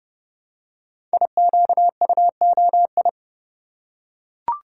Morse code, heard as a keyed tone, about 30 wpm, 700 Hz. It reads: IQUOI
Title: Code Groups - 5 Letters - 30wpm